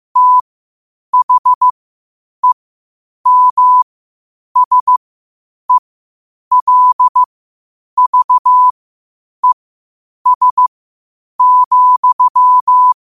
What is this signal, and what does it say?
Morse code, heard as a keyed tone, about 15 wpm, spaced as for 8 wpm, 1000 Hz. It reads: THEMSELVES,
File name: qotdfiles / 8wpm